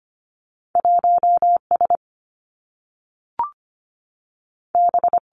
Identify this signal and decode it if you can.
Morse code, heard as a keyed tone, about 25 wpm, 700 Hz. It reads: 1S B
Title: Two Challenging Letter-Number Combinations 25wpm